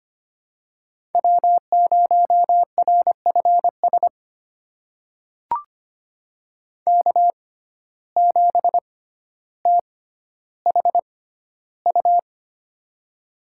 Morse code, heard as a keyed tone, about 25 wpm, spaced as for 13 wpm, 700 Hz. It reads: W0RFS K7THU
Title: USA Call Signs - 5 Characters - 6x Character Spacing - 25wpm